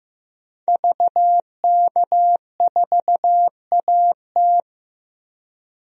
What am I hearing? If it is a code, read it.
Morse code, heard as a keyed tone, about 15 wpm, 700 Hz. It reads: VK4AT
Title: DXCC-100 Call Signs - 5 Characters - 15wpm